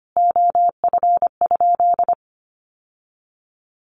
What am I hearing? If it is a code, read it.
Morse code, heard as a keyed tone, about 25 wpm, 700 Hz. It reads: OF?
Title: Sentences from Top 1300 Words 25wpm